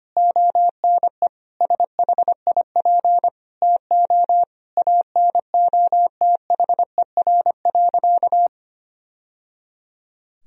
Morse code, heard as a keyed tone, about 25 wpm, 700 Hz. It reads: ONE SHIP TO ANOTHER.
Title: Sentences from Top 1000 Words - Review Part 2 - 25wpm